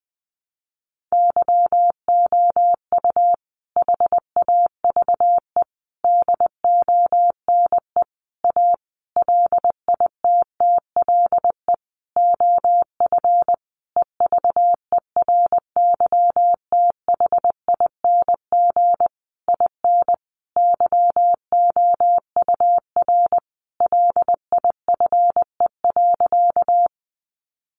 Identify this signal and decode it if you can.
Morse code, heard as a keyed tone, about 20 wpm, 700 Hz. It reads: YOU HAVE DONE A LITTLE OF EVERYTHING IN YOUR LIFE.
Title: Sentences from Top 700 Words 20wpm